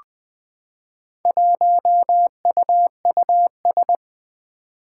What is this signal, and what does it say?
Morse code, heard as a keyed tone, about 20 wpm, 700 Hz. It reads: 1UUS
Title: Four Challenging Letter-Number Combinations 20wpm